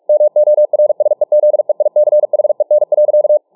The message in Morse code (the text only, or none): MORSEZEICHEN.